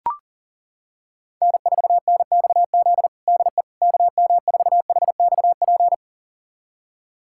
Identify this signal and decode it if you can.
Morse code, heard as a keyed tone, about 37 wpm, 700 Hz. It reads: N4NXZ DE KM4HXP
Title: USA Callsign DE Callsign 40wpm